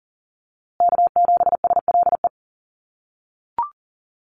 Morse code, heard as a keyed tone, about 40 wpm, 700 Hz. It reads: K7SLE
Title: USA Call Signs Repeated 3x 40wpm